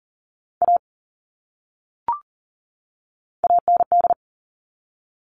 Morse code, heard as a keyed tone, about 40 wpm, 700 Hz. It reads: A AND
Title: Top 500 Words 40wpm